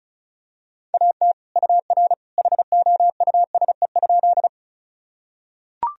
Morse code, heard as a keyed tone, about 36 wpm, 700 Hz. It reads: AT UR HOUSE?
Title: Sentences from Top 200 Words and Abbreviations 35wpm